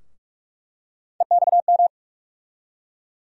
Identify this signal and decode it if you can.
Morse code, heard as a keyed tone, about 46 wpm, 700 Hz. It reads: EXM
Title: USA Call Sign Postfixes - 3 Letters 45wpm